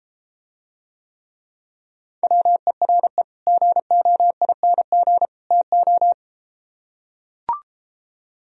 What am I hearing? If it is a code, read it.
Morse code, heard as a keyed tone, about 33 wpm, 700 Hz. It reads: WERE GOING TO